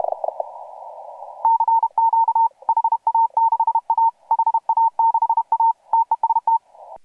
Morse code, heard as a keyed tone, about 32 wpm, 910 Hz. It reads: CQ HA6A HA6A TEST